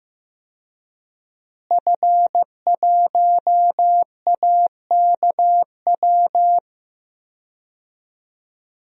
Morse code, heard as a keyed tone, about 15 wpm, 700 Hz. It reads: F1AKW
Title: DXCC-100 Call Signs - 5 Characters - Repeated 3x - 15wpm